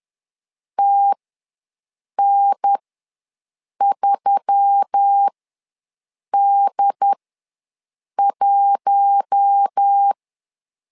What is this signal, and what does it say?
Morse code, heard as a keyed tone, about 10 wpm, spaced as for 6 wpm, 800 Hz. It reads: TN3D1
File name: LEZIONI / 4 Q1S3 ws 10wpm 2437